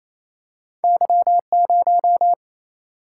Morse code, heard as a keyed tone, about 28 wpm, 700 Hz. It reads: Y0
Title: Single-Letter Single-Number 28wpm